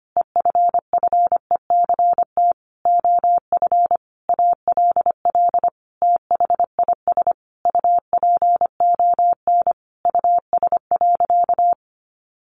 Morse code, heard as a keyed tone, about 25 wpm, 700 Hz. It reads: EFFECT OF ALL THIS UPON US.
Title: Sentences from Top 1400 Words 25wpm